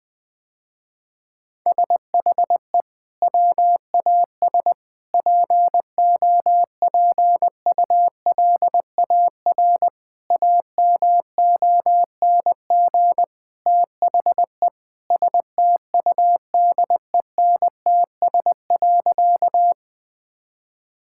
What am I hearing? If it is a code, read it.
Morse code, heard as a keyed tone, about 20 wpm, 700 Hz. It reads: SHE WAS POPULAR AMONG THE STUDENTS.
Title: Sentences with 2-Word Phrases - Spoken Beginning - 20wpm